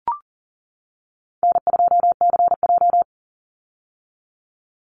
Morse code, heard as a keyed tone, about 40 wpm, 700 Hz. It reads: N2CJ
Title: Easy Call Signs - 40wpm